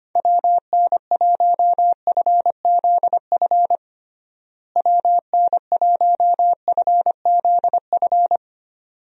Morse code, heard as a keyed tone, about 25 wpm, 700 Hz. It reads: WN1FZF WN1FZF
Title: USA Call Signs - 6 Characters - Repeated 2x - 25wpm